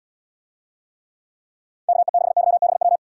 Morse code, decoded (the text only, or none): KE6YBW